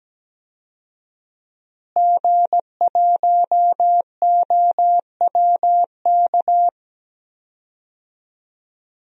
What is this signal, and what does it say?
Morse code, heard as a keyed tone, about 17 wpm, 700 Hz. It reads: G1OWK
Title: DXCC-100 Call Signs - 5 Characters - Repeated 3x - 17wpm